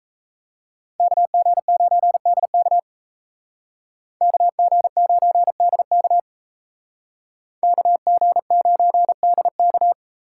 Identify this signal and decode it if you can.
Morse code, heard as a keyed tone, about 34 wpm, 700 Hz. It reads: KG9DK KG9DK KG9DK
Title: Speed Racing - USA 2x2 Call Signs 28wpm